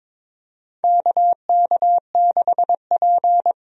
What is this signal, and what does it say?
Morse code, heard as a keyed tone, about 22 wpm, 700 Hz. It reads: KK6P
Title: USA Call Signs - 4 Characters - 22wpm